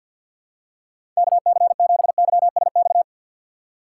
Morse code, heard as a keyed tone, about 47 wpm, 700 Hz. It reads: KC7YSX
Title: USA Call Signs - 6 Characters - 3x Character Spacing - 50wpm